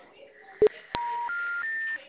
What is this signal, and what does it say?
Special information tone, with speech.
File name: out- -2015- - - .318